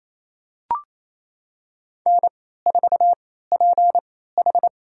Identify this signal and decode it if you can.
Morse code, heard as a keyed tone, about 28 wpm, spaced as for 16 wpm, 700 Hz. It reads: N4PH
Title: USA Call Signs - 4 Characters - 3x Character Spacing - 28wpm